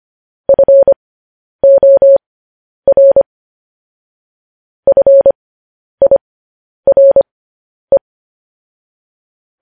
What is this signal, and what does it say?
Morse code, heard as a keyed tone, about 25 wpm, spaced as for 10 wpm, 550 Hz. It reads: FOR FIRE